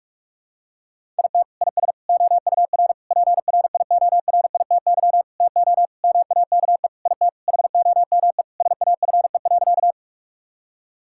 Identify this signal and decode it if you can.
Morse code, heard as a keyed tone, about 41 wpm, 700 Hz. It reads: IT IS OUR PRIORITY TO MAKE IT HOME SAFE.
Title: Sentences from Top 1600 Words 45wpm